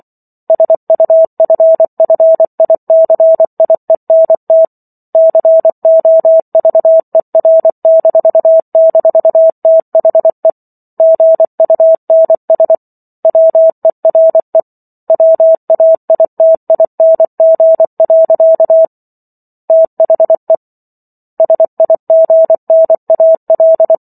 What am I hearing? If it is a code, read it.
Morse code, heard as a keyed tone, about 24 wpm, 650 Hz. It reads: SUFFICIENT COVER--THE GUNS WERE WAITING. THE SIGNAL